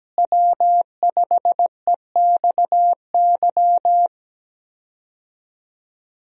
Morse code, heard as a keyed tone, about 17 wpm, 700 Hz. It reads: W5EXY
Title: USA Call Signs - 5 Characters - Repeated 2x - 17wpm